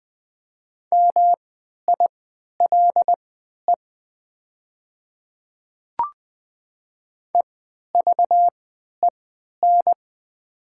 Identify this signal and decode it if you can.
Morse code, heard as a keyed tone, about 20 wpm, spaced as for 11 wpm, 700 Hz. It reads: MILE EVEN
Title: 3x Character Spacing - Top 300 Words - 20wpm